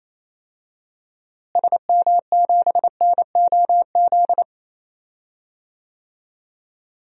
Morse code, heard as a keyed tone, about 27 wpm, 700 Hz. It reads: SM7NOZ
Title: DXCC-100 Call Signs - 6 Characters - Repeated 3x - 28wpm